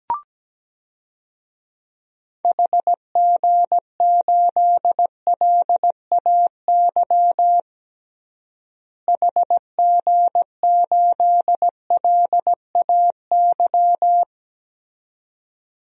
Morse code, heard as a keyed tone, about 17 wpm, 700 Hz. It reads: HG8LAY HG8LAY
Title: DXCC-100 Call Signs - 6 Characters - Repeated 2x - 17wpm